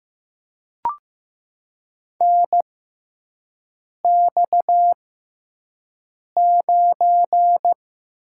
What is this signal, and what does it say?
Morse code, heard as a keyed tone, about 15 wpm, spaced as for 5 wpm, 700 Hz. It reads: NX9
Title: USA Call Signs - 4 Characters - 6x Character Spacing - 15wpm